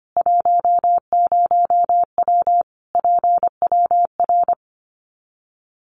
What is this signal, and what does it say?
Morse code, heard as a keyed tone, about 25 wpm, 700 Hz. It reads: 10W PWR
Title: Common QSO Phrases 25wpm